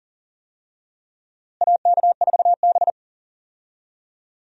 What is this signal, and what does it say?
Morse code, heard as a keyed tone, about 40 wpm, 700 Hz. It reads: AK4B
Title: USA Call Signs - 4 Characters - Repeated 3x - 40wpm